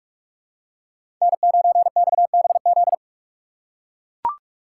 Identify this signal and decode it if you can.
Morse code, heard as a keyed tone, about 44 wpm, 700 Hz. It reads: N9XBZ
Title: USA Call Signs - 5 Characters - 3x Character Spacing - 45wpm